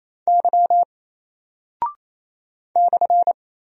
Morse code, heard as a keyed tone, about 28 wpm, 700 Hz. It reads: Y /
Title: Single Character (Letter:Number:Punctuation:Prosign) Alt 28wpm